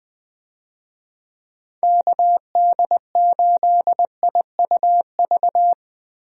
Morse code, heard as a keyed tone, about 20 wpm, 700 Hz. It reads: KD8IUV
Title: USA Call Signs - 6 Characters - Repeated 2x - 20wpm